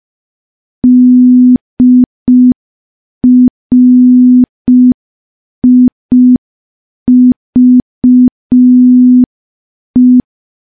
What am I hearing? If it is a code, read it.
Morse code, heard as a keyed tone, about 5 wpm, 250 Hz. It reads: DRIVE
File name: morse / morse-training-05WPM